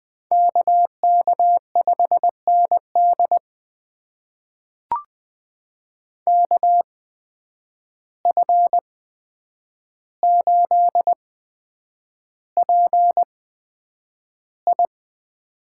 Morse code, heard as a keyed tone, about 20 wpm, spaced as for 9 wpm, 700 Hz. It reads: KK5ND KF8PI
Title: USA Call Signs - 5 Characters - 8x Character Spacing - 20wpm